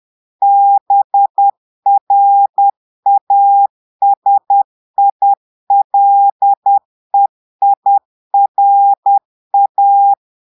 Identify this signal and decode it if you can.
Morse code, heard as a keyed tone, about 10 wpm, 800 Hz. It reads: BRASILEIRA
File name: Portugais - texte en clair 1 - vitesse - 10 mots minute